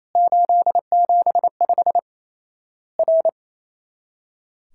Morse code, heard as a keyed tone, about 28 wpm, 700 Hz. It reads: 875 R
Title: Worked All Europe DX Contest - QTC Exchange - 28wpm